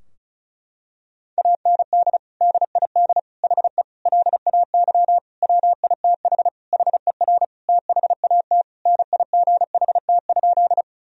Morse code, heard as a keyed tone, about 36 wpm, 700 Hz. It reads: AND DID HE LAY WITH HER THAT NIGHT?
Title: Sentences from Top 1000 Words - Review Part 2 - 35wpm